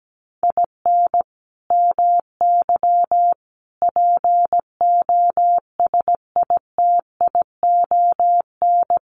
Morse code, heard as a keyed tone, about 17 wpm, 700 Hz. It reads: IN MY POSITION
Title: Sentences from Top 1100 Words 17wpm